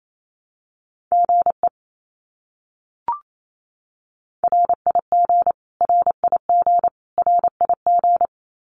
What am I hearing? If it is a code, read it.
Morse code, heard as a keyed tone, about 28 wpm, 700 Hz. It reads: GE RIG RIG RIG
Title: Abbreviations Repeated 3x - 28wpm